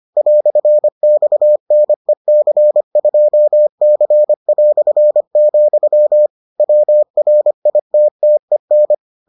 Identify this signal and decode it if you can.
Morse code, heard as a keyed tone, about 25 wpm, 600 Hz. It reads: "XNEC2C", WRITTEN